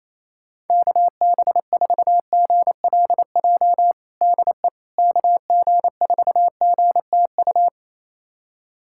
Morse code, heard as a keyed tone, about 28 wpm, 700 Hz. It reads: KB4GLJ DE KG4GTU
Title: USA Callsign DE Callsign - 6 Characters - 28wpm